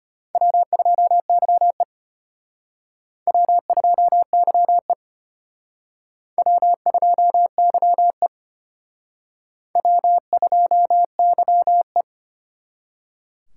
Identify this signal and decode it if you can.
Morse code, heard as a keyed tone, about 32 wpm, 700 Hz. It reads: W2YE W2YE W2YE W2YE